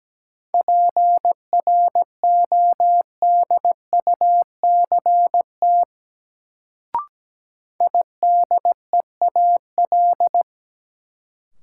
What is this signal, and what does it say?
Morse code, heard as a keyed tone, about 17 wpm, 700 Hz. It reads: PRODUCT IDEAL
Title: Sentences from Top 1400 Words - New Word Drill Alt - 17wpm